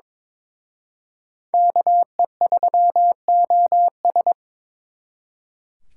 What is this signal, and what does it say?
Morse code, heard as a keyed tone, about 26 wpm, 700 Hz. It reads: KE3OS
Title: Speed Racing - USA 2x2 Call Signs 22wpm